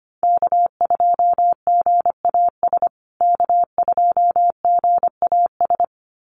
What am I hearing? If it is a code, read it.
Morse code, heard as a keyed tone, about 25 wpm, 700 Hz. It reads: K2GAS K2GAS
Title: USA Call Signs Repeated 2x 25wpm